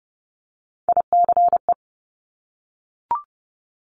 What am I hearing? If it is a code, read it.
Morse code, heard as a keyed tone, about 30 wpm, 700 Hz. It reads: ICE
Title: Multi-letter Phonemes Repeated 3x 30wpm